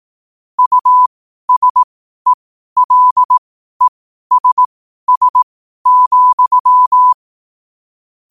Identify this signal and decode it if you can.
Morse code, heard as a keyed tone, about 18 wpm, spaced as for 12 wpm, 1000 Hz. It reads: USELESS,
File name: qotdfiles / 12wpm